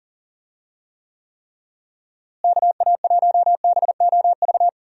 Morse code, heard as a keyed tone, about 40 wpm, 700 Hz. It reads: KA1BOV
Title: USA Call Signs - 6 Characters - Repeated 3x - 40wpm